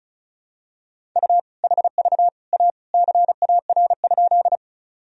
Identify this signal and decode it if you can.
Morse code, heard as a keyed tone, about 34 wpm, 700 Hz. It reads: U HV A CAR?
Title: Sentences from Top 300 Words and Abbreviations 35wpm